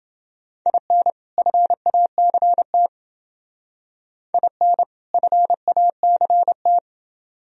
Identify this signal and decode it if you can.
Morse code, heard as a keyed tone, about 28 wpm, 700 Hz. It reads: IN FACT IN FACT